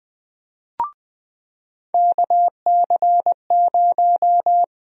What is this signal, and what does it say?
Morse code, heard as a keyed tone, about 20 wpm, 700 Hz. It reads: KC0
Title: USA Call Sign Prefixes - 20wpm